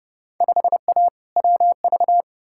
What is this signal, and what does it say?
Morse code, heard as a keyed tone, about 30 wpm, 700 Hz. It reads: 5A WV
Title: Field Day Exchanges - 30wpm